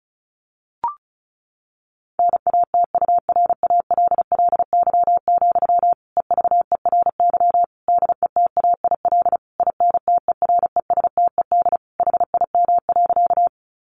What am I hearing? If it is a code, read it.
Morse code, heard as a keyed tone, about 34 wpm, 700 Hz. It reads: NATURALLY, EVERY DETAIL INTERESTED HIM.
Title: Sentences from Top 1500 Words 35wpm